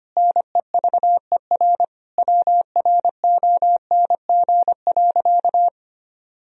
Morse code, heard as a keyed tone, about 25 wpm, 700 Hz. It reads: NEVER WRONG.
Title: Sentences from Top 1000 Words - Review Part 3 - 25wpm